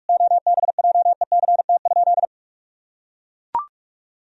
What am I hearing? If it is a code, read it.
Morse code, heard as a keyed tone, about 41 wpm, 700 Hz. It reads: OBJECT?
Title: Sentences from Top 1000 Words - Review Part 4 - 45wpm